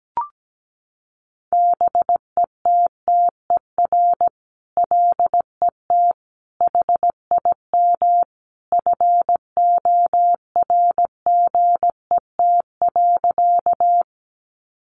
Morse code, heard as a keyed tone, about 17 wpm, 700 Hz. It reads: BETTER LET HIM FORGET.
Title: Sentences from Top 1200 Words 17wpm